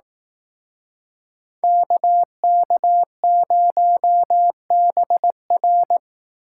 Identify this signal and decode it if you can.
Morse code, heard as a keyed tone, about 18 wpm, 700 Hz. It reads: KK0BR
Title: Speed Racing - USA 2x2 Call Signs 15wpm